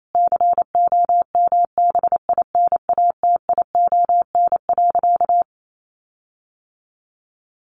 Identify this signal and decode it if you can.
Morse code, heard as a keyed tone, about 28 wpm, 700 Hz. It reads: COMBINATION.
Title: Sentences from Top 1300 Words 28wpm